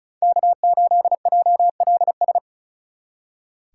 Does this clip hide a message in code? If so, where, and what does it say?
Yes: Morse code from 0.2 to 2.4 s, heard as a keyed tone, about 47 wpm, 700 Hz. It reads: K8JLS